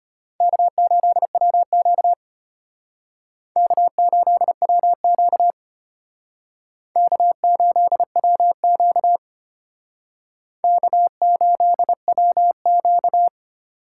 Morse code, heard as a keyed tone, about 30 wpm, 700 Hz. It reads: K8WQ K8WQ K8WQ K8WQ
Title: Speed Racing - USA Call Signs 25wpm